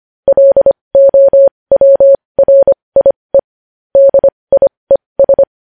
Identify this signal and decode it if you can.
Morse code, heard as a keyed tone, about 25 wpm, spaced as for 20 wpm, 550 Hz. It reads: LOWRIE DIES
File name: podcast / News headlines-1x-25x20